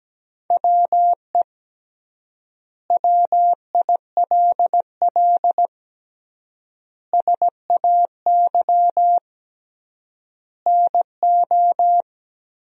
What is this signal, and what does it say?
Morse code, heard as a keyed tone, about 17 wpm, 700 Hz. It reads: WE WILL SAY NO
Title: 3x Word-Spacing - Sentences from Top 200 Words - 17wpm